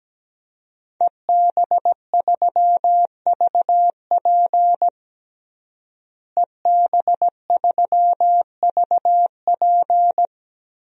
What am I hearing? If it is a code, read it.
Morse code, heard as a keyed tone, about 17 wpm, 700 Hz. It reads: EB3VP EB3VP
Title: DXCC-100 Call Signs - 5 Characters - Repeated 3x - 17wpm